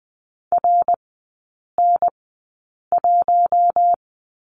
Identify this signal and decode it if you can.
Morse code, heard as a keyed tone, about 20 wpm, spaced as for 8 wpm, 690 Hz. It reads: RN1
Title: 087 Lesson 15e - Number 5 - Three Characters 20wpm